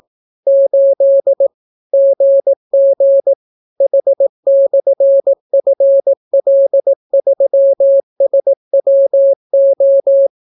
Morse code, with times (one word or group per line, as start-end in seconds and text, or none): 0.5-1.5 8
1.9-3.3 GG
3.8-10.3 H/FL3SWO